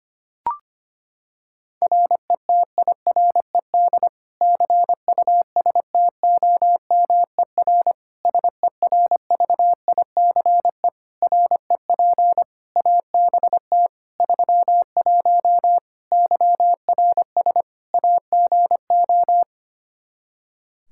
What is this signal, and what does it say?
Morse code, heard as a keyed tone, about 25 wpm, 700 Hz. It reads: RETIRED CUSTOMER SERVICE REP ABT 31 YRS AGO
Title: Practice QSO - Exchange 3 - More Info - 25wpm